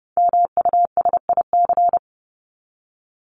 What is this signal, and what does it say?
Morse code, heard as a keyed tone, about 29 wpm, 700 Hz. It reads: MUSIC
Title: Top 300 Words 30wpm